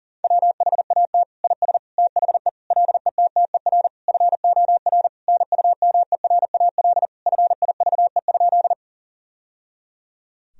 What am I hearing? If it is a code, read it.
Morse code, heard as a keyed tone, about 40 wpm, 700 Hz. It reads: WHAT IS THE LETTER FOR NUMERAL FIVE?
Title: Sentences from Top 1000 Words - Review Part 2 - 40wpm